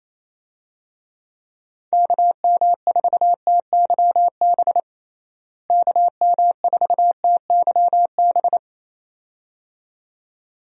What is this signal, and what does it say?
Morse code, heard as a keyed tone, about 28 wpm, 700 Hz. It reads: KM4TYB KM4TYB
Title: USA Call Signs - 6 Characters - Repeated 2x - 28wpm